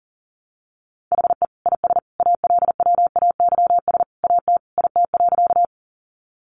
Morse code, heard as a keyed tone, about 40 wpm, 700 Hz. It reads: HE IS ALWAYS AT IT.
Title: Sentences from Top 300 Words 40wpm